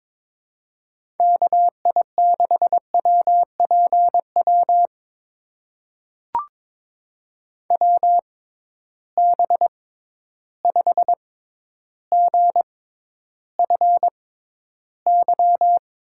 Morse code, heard as a keyed tone, about 22 wpm, spaced as for 11 wpm, 700 Hz. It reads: KI6WPW WB5GFY
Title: USA Call Signs - 6 Characters - 6x Character Spacing - 22wpm